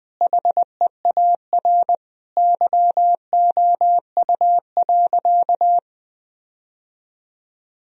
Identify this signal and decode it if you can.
Morse code, heard as a keyed tone, about 21 wpm, 700 Hz. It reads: HEAR YOU.
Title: Sentences from Top 1000 Words - Review Part 1 - 20wpm